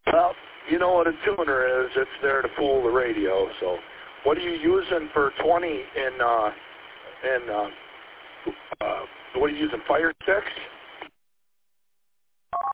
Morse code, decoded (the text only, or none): none